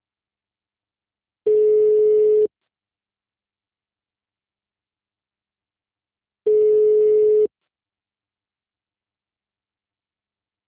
Ringback tone (Europe, 425 Hz).